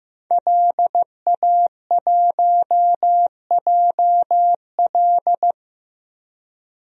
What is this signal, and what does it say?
Morse code, heard as a keyed tone, about 15 wpm, 700 Hz. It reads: LA1JL
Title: DXCC-100 Call Signs - 5 Characters - Repeated 2x - 15wpm